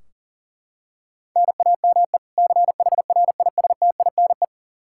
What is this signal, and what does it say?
Morse code, heard as a keyed tone, about 40 wpm, 700 Hz. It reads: NAME CHRISTINE
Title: Practice QSO - Exchange 1 - Beginning - 40wpm